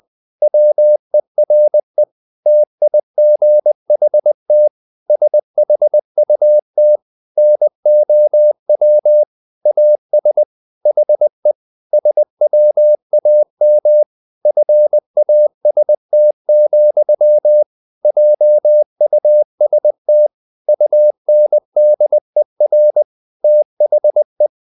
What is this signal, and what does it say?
Morse code, heard as a keyed tone, about 20 wpm, 600 Hz. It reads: WERE TIGHT SHUT NOW AS HE SWAM FAST, JUST UNDER THE
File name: the old man and the sea part 2 20wpm